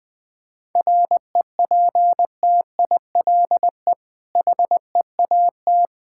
Morse code, heard as a keyed tone, about 20 wpm, 690 Hz. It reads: REPTILE HEAT
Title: Difficult Words with Hints - 20wpm